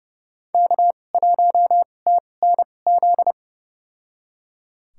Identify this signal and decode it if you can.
Morse code, heard as a keyed tone, about 30 wpm, spaced as for 22 wpm, 700 Hz. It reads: K1TNZ
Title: USA Call Signs - 5 Characters - 2x Character Spacing - 30wpm